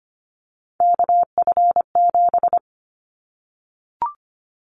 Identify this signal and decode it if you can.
Morse code, heard as a keyed tone, about 25 wpm, 700 Hz. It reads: KF7
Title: Callsign Prefixes for Top 25 DXCC - Repeated 3x - 25wpm